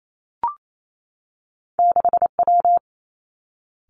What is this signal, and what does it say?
Morse code, heard as a keyed tone, about 28 wpm, 700 Hz. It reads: 6W
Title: Two Challenging Letter-Number Combinations 28wpm